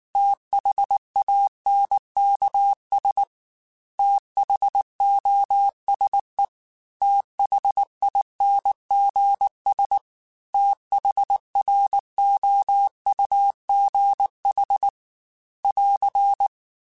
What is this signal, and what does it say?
Morse code, heard as a keyed tone, about 19 wpm, 800 Hz. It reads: THANKS THOSE THINGS THROUGH <AR>
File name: TH Runs / TH Runs mp3s / TH Runs @18